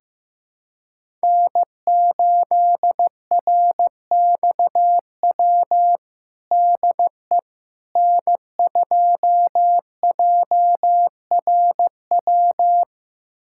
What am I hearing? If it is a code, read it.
Morse code, heard as a keyed tone, about 16 wpm, 700 Hz. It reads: N8RXW DE N2JRW
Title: USA Callsign DE Callsign - 5 Characters - 15wpm